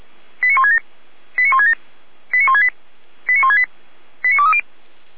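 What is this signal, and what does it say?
Five-tone selective call (ZVEI), heard as quick runs of five tones, tones of about 70 ms. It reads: 89167 89167 89168 89168 89230
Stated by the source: ZVEI five-tone selective call 89119, received with an SDR.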